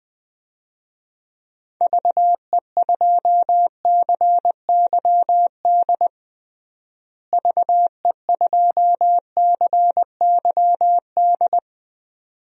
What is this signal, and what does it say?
Morse code, heard as a keyed tone, about 20 wpm, 700 Hz. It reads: VE2CYD VE2CYD
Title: DXCC-100 Call Signs - 6 Characters - Repeated 2x - 20wpm